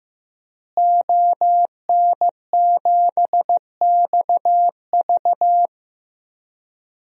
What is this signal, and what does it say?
Morse code, heard as a keyed tone, about 16 wpm, 700 Hz. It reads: ON7XV
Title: DXCC-100 Call Signs - 5 Characters - Repeated 2x - 15wpm